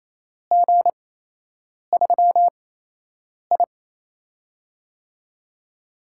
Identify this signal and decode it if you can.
Morse code, heard as a keyed tone, about 28 wpm, spaced as for 8 wpm, 700 Hz. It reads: G3I
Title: USA Call Signs - 8x Character Spacing - 28wpm